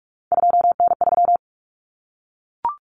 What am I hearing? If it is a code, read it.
Morse code, heard as a keyed tone, about 44 wpm, 700 Hz. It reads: 2N3
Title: Common Cut Numbers - 3-Digit 45wpm